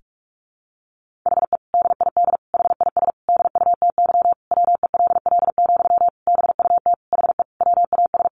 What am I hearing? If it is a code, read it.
Morse code, heard as a keyed tone, about 44 wpm, 700 Hz. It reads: HE DID HIS DUTY WELL, BUT HE WAS
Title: Sentences from Top 1500 Words 45wpm